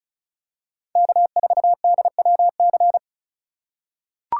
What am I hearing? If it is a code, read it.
Morse code, heard as a keyed tone, about 36 wpm, 700 Hz. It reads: K4DWC
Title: USA Call Signs - 5 Characters - 4x Character Spacing - 35wpm